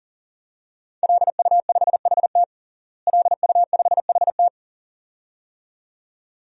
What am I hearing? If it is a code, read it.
Morse code, heard as a keyed tone, about 40 wpm, 700 Hz. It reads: LU5HT LU5HT
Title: DXCC-100 Call Signs - 5 Characters - Repeated 2x - 40wpm